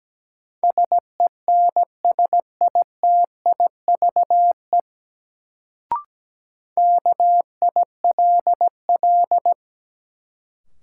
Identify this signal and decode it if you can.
Morse code, heard as a keyed tone, about 17 wpm, 700 Hz. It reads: SENSITIVE KILL
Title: Sentences from Top 1400 Words - New Word Drill - 17wpm